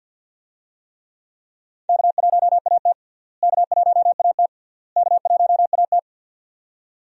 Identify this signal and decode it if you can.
Morse code, heard as a keyed tone, about 53 wpm, 700 Hz. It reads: K1AT K1AT K1AT
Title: USA Call Signs - 4 Characters - Repeated 3x - 50wpm